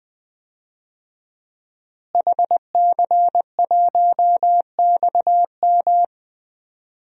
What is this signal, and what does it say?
Morse code, heard as a keyed tone, about 20 wpm, 700 Hz. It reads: HC1XM
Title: DXCC-100 Call Signs - 5 Characters - Repeated 3x - 20wpm